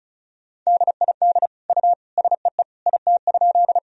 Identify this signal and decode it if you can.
Morse code, heard as a keyed tone, about 34 wpm, 700 Hz. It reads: DID U SEE IT?
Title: Sentences from Top 100 Words and Abbreviations 35wpm